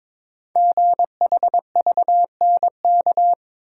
Morse code, heard as a keyed tone, about 22 wpm, 700 Hz. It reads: GHVNK
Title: Code Groups - 5 Letters - 22wpm